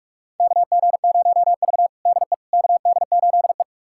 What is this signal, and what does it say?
Morse code, heard as a keyed tone, about 46 wpm, 700 Hz. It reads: KG0V DE KD8E
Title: USA Callsign DE Callsign - 4 Characters - 45wpm